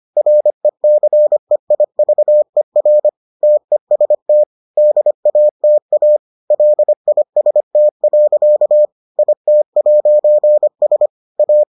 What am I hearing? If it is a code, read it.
Morse code, heard as a keyed tone, about 25 wpm, 600 Hz. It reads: RECEIVER TEST DATA LIST. IT'S A